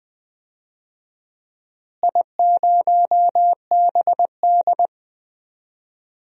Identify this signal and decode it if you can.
Morse code, heard as a keyed tone, about 20 wpm, 700 Hz. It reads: I0BD
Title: DXCC-100 Call Signs - 4 Characters - Repeated 2x - 20wpm